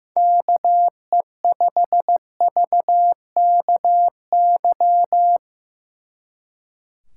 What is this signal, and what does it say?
Morse code, heard as a keyed tone, about 17 wpm, 700 Hz. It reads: KE5VKY
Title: Speed Racing - USA 2x3 Call Signs 15wpm